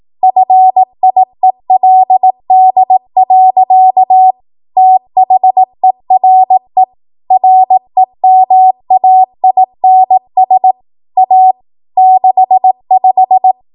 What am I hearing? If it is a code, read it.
Morse code, heard as a keyed tone, about 18 wpm, 750 Hz. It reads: FIELD. THERE REMAINS A 65
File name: Morse / 18WPM